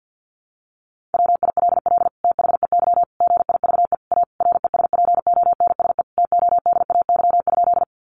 Morse code, heard as a keyed tone, about 50 wpm, 700 Hz. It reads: WILL THEY GIVE A RESPONSE TODAY?
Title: Sentences from Top 1400 Words 50wpm